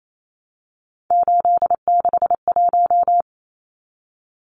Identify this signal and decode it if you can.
Morse code, heard as a keyed tone, about 28 wpm, 700 Hz. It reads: 861
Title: Numbers - 3-Digit 28wpm